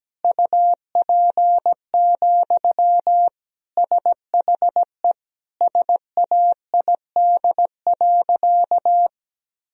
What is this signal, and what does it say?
Morse code, heard as a keyed tone, about 17 wpm, 700 Hz. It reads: UP, SHE SAID.
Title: Sentences from Top 1000 Words - Review Part 1 - 17wpm